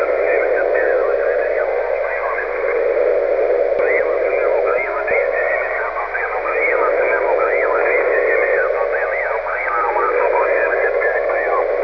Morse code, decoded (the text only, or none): none